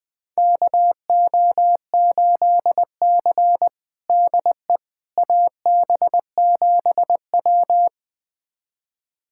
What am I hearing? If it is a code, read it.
Morse code, heard as a keyed tone, about 20 wpm, 700 Hz. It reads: KO8C DE AB7W
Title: USA Callsign DE Callsign - 4 Characters - 20wpm